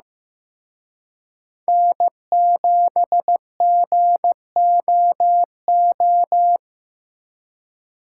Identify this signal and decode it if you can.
Morse code, heard as a keyed tone, about 18 wpm, 700 Hz. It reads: N7GOO
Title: Speed Racing - USA 1x3 Call Signs 15wpm